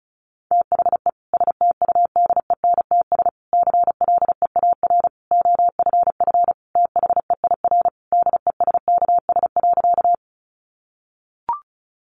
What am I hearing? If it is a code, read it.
Morse code, heard as a keyed tone, about 34 wpm, 700 Hz. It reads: THE STUDENTS CLEAR OFF THEIR DESKS.
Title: Additional Easy Phrases 35wpm